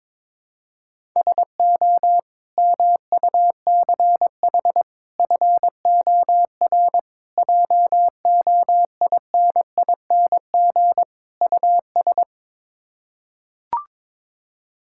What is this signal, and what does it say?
Morse code, heard as a keyed tone, about 22 wpm, 700 Hz. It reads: SO MUCH FOR JOINING US
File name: practice / 22wpm / Build A Phrase - 5-Word Phrases - 22wpm